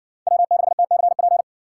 Morse code, heard as a keyed tone, about 60 wpm, 700 Hz. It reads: W6TCP